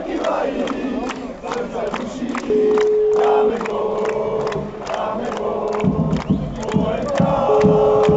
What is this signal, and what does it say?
Ringback tone (Europe, 425 Hz), with noise.